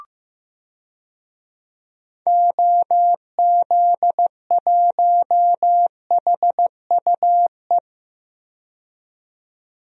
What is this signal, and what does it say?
Morse code, heard as a keyed tone, about 15 wpm, 700 Hz. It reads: OZ1HUE
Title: DXCC-100 Call Signs - 6 Characters - 15wpm